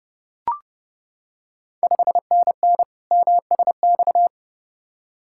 Morse code, heard as a keyed tone, about 30 wpm, 700 Hz. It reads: 5NN MSX
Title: Ontario QSO Party Exchange 30wpm